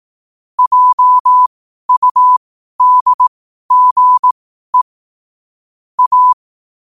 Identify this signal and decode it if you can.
Morse code, heard as a keyed tone, about 18 wpm, spaced as for 12 wpm, 1000 Hz. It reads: JUDGE A
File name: qotdfiles / 12wpm